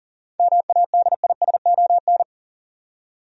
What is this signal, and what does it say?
Morse code, heard as a keyed tone, about 40 wpm, 700 Hz. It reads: MADISON
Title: USA Common Names - 40wpm